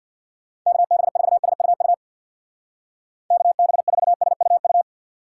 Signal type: Morse code

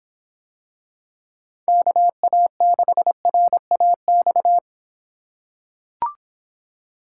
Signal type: Morse code

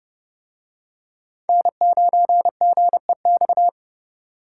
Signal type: Morse code

